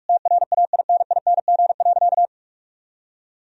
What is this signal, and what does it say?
Morse code, heard as a keyed tone, about 46 wpm, 700 Hz. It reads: TRAINING.
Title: Sentences from Top 1000 Words - Review Part 4 - 45wpm